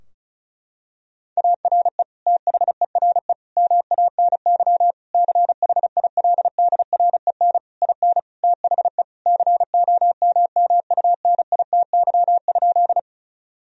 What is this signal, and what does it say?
Morse code, heard as a keyed tone, about 36 wpm, 700 Hz. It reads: ARE THERE MANY CHILDREN IN THE COMMUNITY?
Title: Sentences from Top 1000 Words - Review Part 4 - 35wpm